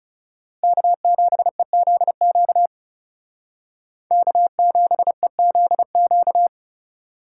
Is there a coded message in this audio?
Yes: Morse code from 0.6 to 6.5 s, heard as a keyed tone, about 38 wpm, 700 Hz. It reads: K7EZQ K7EZQ